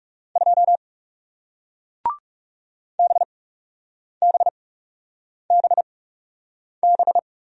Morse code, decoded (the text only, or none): 2 BBBB